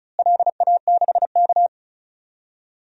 Morse code, heard as a keyed tone, about 34 wpm, 700 Hz. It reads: LA6K
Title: DXCC-100 Call Signs - 4 Characters - Repeated 2x - 35wpm